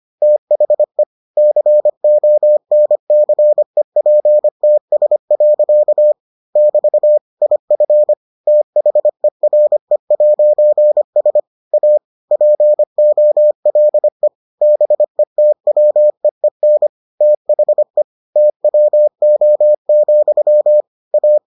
Morse code, heard as a keyed tone, about 25 wpm, 600 Hz. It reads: THE CONCEPTS. <BT> IF THERE'S A POLE BETWEEN THE TWO, A